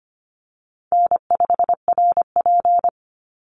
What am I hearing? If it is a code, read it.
Morse code, heard as a keyed tone, about 25 wpm, 700 Hz. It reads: N5RP
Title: Call Signs - 25wpm